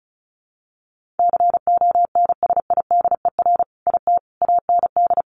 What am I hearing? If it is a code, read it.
Morse code, heard as a keyed tone, about 34 wpm, 700 Hz. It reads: CONSIDER IT AND
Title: Sentences from Top 1400 Words 35wpm